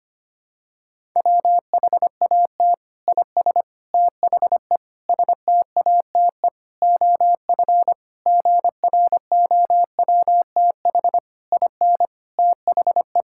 Morse code, heard as a keyed tone, about 25 wpm, 700 Hz. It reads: WHAT IS THE STATE OF GROWTH IN THE
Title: Sentences from Top 1000 Words - Review Part 5 - 25wpm